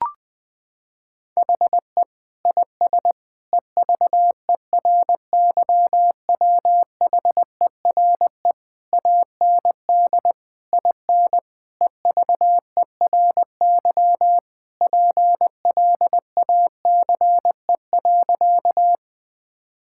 Morse code, heard as a keyed tone, about 21 wpm, 700 Hz. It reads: HE IS EVERYWHERE AND IN EVERY PLACE.
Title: Sentences from Top 1600 Words 20wpm